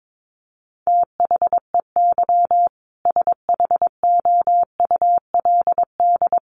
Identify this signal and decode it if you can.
Morse code, heard as a keyed tone, about 22 wpm, 700 Hz. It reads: THEY SHOULD
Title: Sentences from Top 1500 Words 22wpm